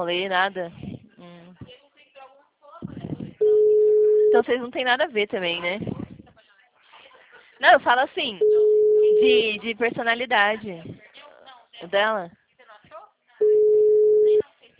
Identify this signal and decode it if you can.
Ringback tone (Europe, 425 Hz), with speech.
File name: out- -2003- - - .41515